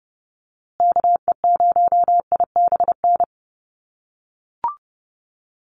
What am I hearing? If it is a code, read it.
Morse code, heard as a keyed tone, about 29 wpm, 700 Hz. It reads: KE0IBN
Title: USA Call Signs Repeated 3x 30wpm